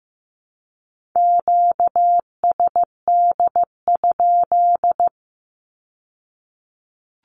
Morse code, heard as a keyed tone, about 15 wpm, 700 Hz. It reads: QSD?
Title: Q-Codes with Spoken Meaning - 15wpm